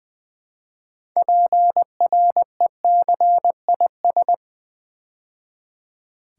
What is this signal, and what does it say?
Morse code, heard as a keyed tone, about 20 wpm, 700 Hz. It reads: PRECIS
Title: Build A Word - 3 Letters At A Time - 9-Letter Words - 20wpm